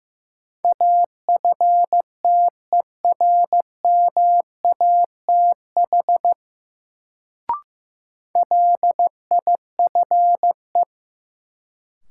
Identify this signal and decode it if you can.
Morse code, heard as a keyed tone, about 15 wpm, 700 Hz. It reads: AFTERMATH LIFE
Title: Sets of 5 Words - New Word Drill - 15wpm